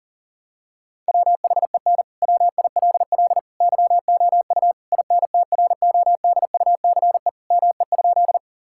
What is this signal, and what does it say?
Morse code, heard as a keyed tone, about 37 wpm, 700 Hz. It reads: WHEN WILL YOU INTRODUCE ME?
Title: Sentences from Top 1600 Words 40wpm